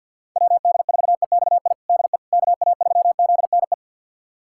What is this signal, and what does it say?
Morse code, heard as a keyed tone, about 51 wpm, 700 Hz. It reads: WD4EXI DE KA3ZNE
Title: USA Callsign DE Callsign - 6 Characters - 50wpm